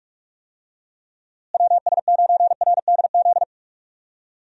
Morse code, heard as a keyed tone, about 46 wpm, 700 Hz. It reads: WS9RDZ
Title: USA Call Signs - 6 Characters - Repeated 2x - 45wpm